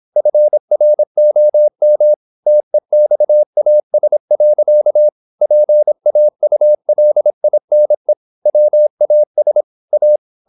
Morse code, heard as a keyed tone, about 26 wpm, 600 Hz. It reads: FROM TEXAS. PAULINE WAS A
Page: Never 26wpm | CW med Gnister